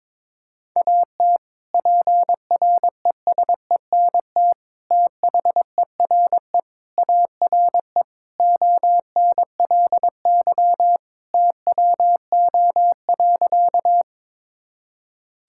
Morse code, heard as a keyed tone, about 22 wpm, 700 Hz. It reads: AT PRESENT THERE ARE ONLY TWO.
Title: Sentences from Top 1000 Words - Review Part 5 - 22wpm